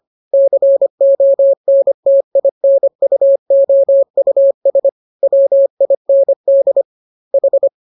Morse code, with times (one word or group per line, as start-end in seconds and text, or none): 0.3-4.9 CONTINUOUS
5.2-6.8 WIND
7.3-7.7 H